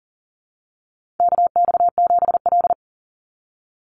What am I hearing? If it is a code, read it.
Morse code, heard as a keyed tone, about 40 wpm, 700 Hz. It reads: KX7L
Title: NA SOTA Chaser Callsigns 40wpm